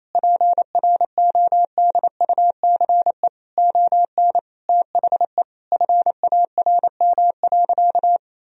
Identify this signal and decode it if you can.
Morse code, heard as a keyed tone, about 28 wpm, 700 Hz. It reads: PRODUCE ON THE FARM.
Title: Sentences from Top 1000 Words - Review Part 3 - 28wpm